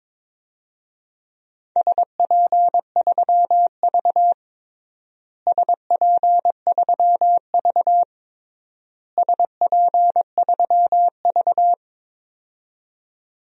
Morse code, heard as a keyed tone, about 22 wpm, 700 Hz. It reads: SP3V SP3V SP3V
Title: DXCC-100 Call Signs - 4 Characters - Repeated 3x - 22wpm